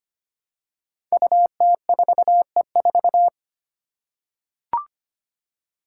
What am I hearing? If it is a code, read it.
Morse code, heard as a keyed tone, about 25 wpm, 700 Hz. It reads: UT4E4
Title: Five Challenging Letter-Number Combinations 25wpm